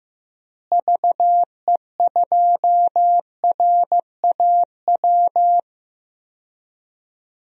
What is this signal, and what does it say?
Morse code, heard as a keyed tone, about 15 wpm, 700 Hz. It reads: VE2RAW
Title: DXCC-100 Call Signs - 6 Characters - Repeated 3x - 15wpm